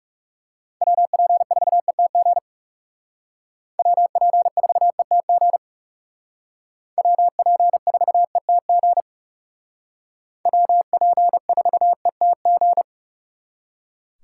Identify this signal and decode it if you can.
Morse code, heard as a keyed tone, about 37 wpm, 700 Hz. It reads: WP4ETG WP4ETG WP4ETG WP4ETG
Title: Speed Racing - USA 2x3 Call Signs 30wpm